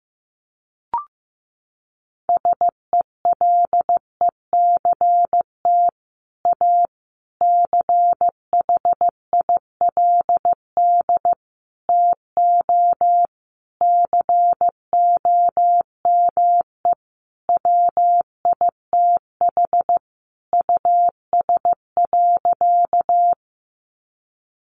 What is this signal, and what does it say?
Morse code, heard as a keyed tone, about 15 wpm, 700 Hz. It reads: SELECT A CHILD TO COME WITH US.
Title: Sentences from Top 1300 Words 15wpm